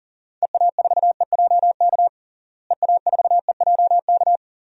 Morse code, heard as a keyed tone, about 40 wpm, 700 Hz. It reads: EA4EJK EA4EJK
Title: DXCC-100 Call Signs - 6 Characters - Repeated 2x - 40wpm